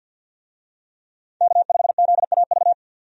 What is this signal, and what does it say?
Morse code, heard as a keyed tone, about 52 wpm, 700 Hz. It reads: K5ZAV
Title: USA Call Signs - 5 Characters - 50wpm